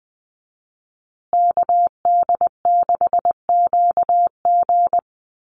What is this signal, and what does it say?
Morse code, heard as a keyed tone, about 20 wpm, 700 Hz. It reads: KD6QG
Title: USA Call Signs Repeated 3x 20wpm